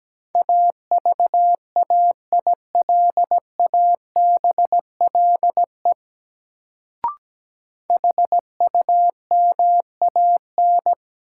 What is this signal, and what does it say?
Morse code, heard as a keyed tone, about 17 wpm, 700 Hz. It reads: AVAILABLE HUMAN
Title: Sentences from Top 600 Words - New Word Drill - 17wpm